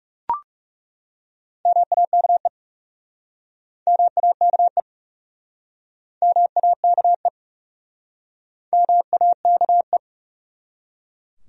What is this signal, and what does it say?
Morse code, heard as a keyed tone, about 41 wpm, 700 Hz. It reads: MAKE MAKE MAKE MAKE